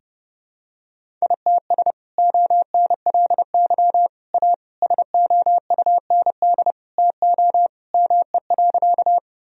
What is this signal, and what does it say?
Morse code, heard as a keyed tone, about 30 wpm, 700 Hz. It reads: ITS ONLY A SOUND TO ME.
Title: Sentences from Top 1000 Words - Review Part 1 - 30wpm